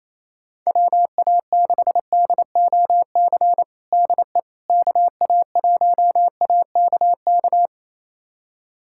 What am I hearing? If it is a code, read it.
Morse code, heard as a keyed tone, about 27 wpm, 700 Hz. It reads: WA6DOC DE KA1AKK
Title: USA Callsign DE Callsign - 6 Characters - 28wpm